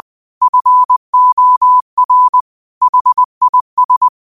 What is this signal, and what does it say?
Morse code, heard as a keyed tone, about 20 wpm, 1000 Hz. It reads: FOR HIS
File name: qotdfiles / 20wpm